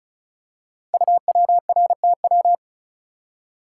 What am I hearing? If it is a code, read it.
Morse code, heard as a keyed tone, about 36 wpm, 690 Hz. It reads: UWRTW
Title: Code Groups - 5 Letters - 35wpm